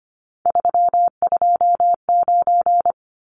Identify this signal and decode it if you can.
Morse code, heard as a keyed tone, about 25 wpm, 700 Hz. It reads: 329
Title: Numbers - 3-Digit 25wpm